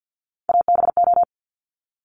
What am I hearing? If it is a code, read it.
Morse code, heard as a keyed tone, about 50 wpm, 700 Hz. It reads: ABO
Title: Top Three Letter Combinations in Words 50wpm